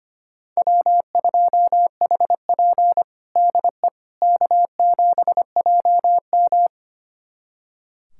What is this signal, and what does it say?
Morse code, heard as a keyed tone, about 25 wpm, 700 Hz. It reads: W2HP DE K7JM
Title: USA Callsign DE Callsign - 4 Characters - 25wpm